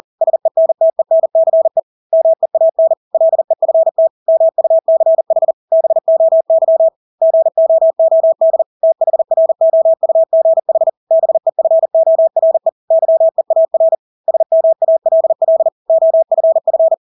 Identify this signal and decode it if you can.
Morse code, heard as a keyed tone, about 40 wpm, 650 Hz. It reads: SENTENCE MEAN LEFT MUCH BOY GOOD THROUGH BEFORE YEAR SMALL OFF
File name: Engelske ord 40-40 wpm